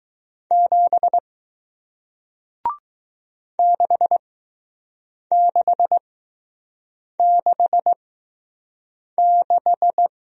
Morse code, heard as a keyed tone, about 20 wpm, spaced as for 6 wpm, 700 Hz. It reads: 7 6666